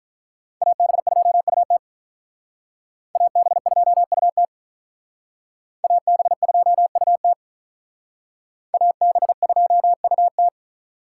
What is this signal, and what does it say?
Morse code, heard as a keyed tone, about 43 wpm, 700 Hz. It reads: AB2UT AB2UT AB2UT AB2UT
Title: Speed Racing - USA 2x2 Call Signs 35wpm